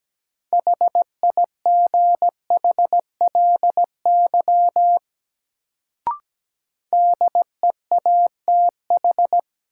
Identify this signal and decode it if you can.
Morse code, heard as a keyed tone, about 17 wpm, 700 Hz. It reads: HIGHLY DEATH
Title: Sentences from Top 1100 Words - New Word Drill - 17wpm